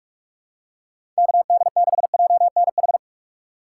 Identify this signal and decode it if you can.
Morse code, heard as a keyed tone, about 45 wpm, 700 Hz. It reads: KD6JNH
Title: USA Call Signs - 6 Characters - 3x Character Spacing - 45wpm